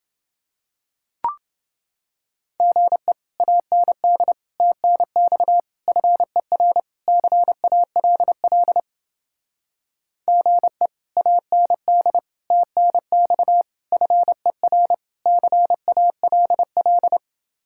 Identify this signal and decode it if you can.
Morse code, heard as a keyed tone, about 29 wpm, 700 Hz. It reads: GE AND TNX FER CALL GE AND TNX FER CALL